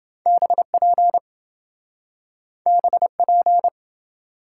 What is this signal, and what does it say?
Morse code, heard as a keyed tone, about 27 wpm, 700 Hz. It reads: BP BP